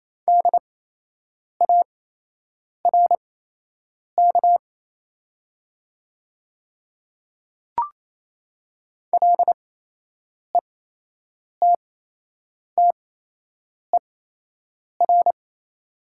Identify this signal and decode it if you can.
Morse code, heard as a keyed tone, about 28 wpm, spaced as for 8 wpm, 700 Hz. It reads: DARK LETTER
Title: 8x Character Spacing - Top 500 Words - 28wpm